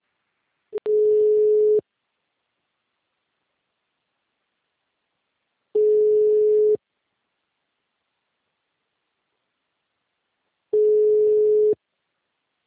Ringback tone (Europe, 425 Hz).
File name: out- -2017- - - .7896